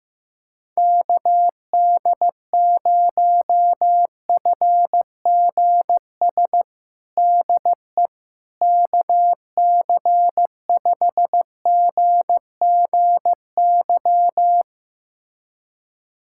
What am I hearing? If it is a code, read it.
Morse code, heard as a keyed tone, about 15 wpm, 700 Hz. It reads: KD0FGS DE KC5GGY
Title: USA Callsign DE Callsign - 6 Characters - 15wpm